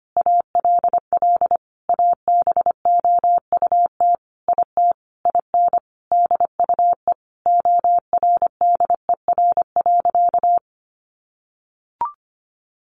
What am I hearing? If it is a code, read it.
Morse code, heard as a keyed tone, about 25 wpm, 700 Hz. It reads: ALL ABOUT IT IN DUE ORDER.
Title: Sentences from Top 900 Words 25wpm